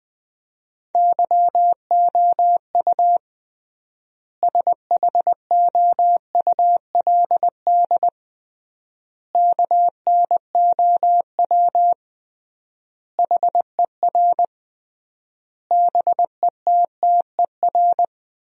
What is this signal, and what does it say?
Morse code, heard as a keyed tone, about 21 wpm, 700 Hz. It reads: YOU SHOULD KNOW HER BETTER
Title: 3x Word-Spacing - Sentences from Top 400 Words - 20wpm